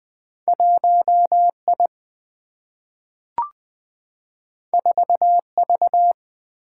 Morse code, heard as a keyed tone, about 20 wpm, 700 Hz. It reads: 1I 4V
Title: Two Letter-Number Combinations - 20wpm